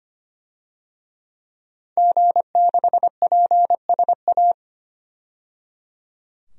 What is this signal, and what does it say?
Morse code, heard as a keyed tone, about 25 wpm, 700 Hz. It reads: G6PSA
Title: DXCC-100 Call Signs - 5 Characters - 25wpm